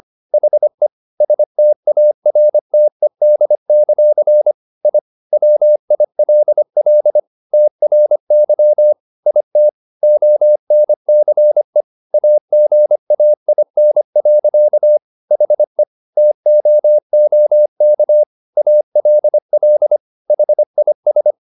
Morse code, heard as a keyed tone, about 25 wpm, 600 Hz. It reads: HE STARTED; I WILL TRY IT ONCE AGAIN. HE TOOK ALL HIS